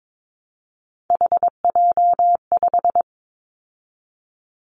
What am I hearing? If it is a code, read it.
Morse code, heard as a keyed tone, about 22 wpm, 700 Hz. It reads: HJ5
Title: Callsign Prefixes for Top 100 DXCC - 22wpm